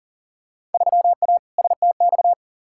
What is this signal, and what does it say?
Morse code, heard as a keyed tone, about 40 wpm, 700 Hz. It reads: 3A STX
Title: Field Day Exchanges - 40wpm